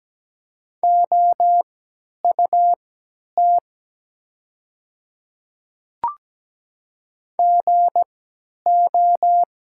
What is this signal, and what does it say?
Morse code, heard as a keyed tone, about 17 wpm, spaced as for 10 wpm, 700 Hz. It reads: OUT GO
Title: 3x Character Spacing - Top 100 Words - 17wpm